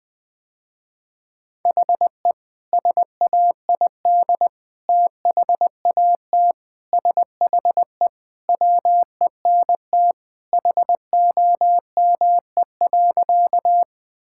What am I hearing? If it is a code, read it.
Morse code, heard as a keyed tone, about 21 wpm, 700 Hz. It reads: HE SAID THAT SHE WENT HOME.
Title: Sentences with 2-Word Phrases - Spoken Beginning - 20wpm